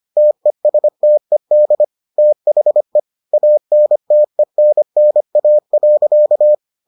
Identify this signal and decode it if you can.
Morse code, heard as a keyed tone, about 25 wpm, 600 Hz. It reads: TESTED THE ANTENNA.